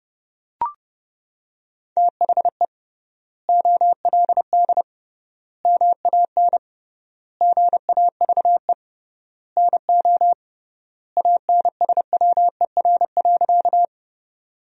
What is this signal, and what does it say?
Morse code, heard as a keyed tone, about 32 wpm, 700 Hz. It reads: THE OLD MAN GAVE NO ANSWER.
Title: 3x Word-Spacing - Sentences from Top 500 Words - 30wpm